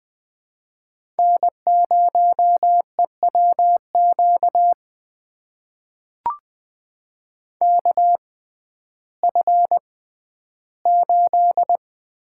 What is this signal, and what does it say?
Morse code, heard as a keyed tone, about 20 wpm, spaced as for 12 wpm, 700 Hz. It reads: N0EWQ KF8
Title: USA Call Signs - 5 Characters - 6x Character Spacing - 20wpm